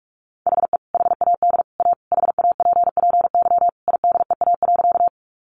Morse code, heard as a keyed tone, about 44 wpm, 700 Hz. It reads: HE HAD A HAPPY IDEA.
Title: Sentences from Top 1000 Words 45wpm